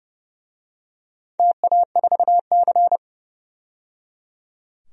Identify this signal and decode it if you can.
Morse code, heard as a keyed tone, about 30 wpm, 700 Hz. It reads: TA4C
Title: DXCC-100 Call Signs - 4 Characters - 30wpm